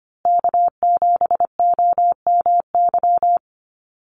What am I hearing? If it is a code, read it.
Morse code, heard as a keyed tone, about 25 wpm, 700 Hz. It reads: K7OMY
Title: USA Call Signs Repeated 3x 25wpm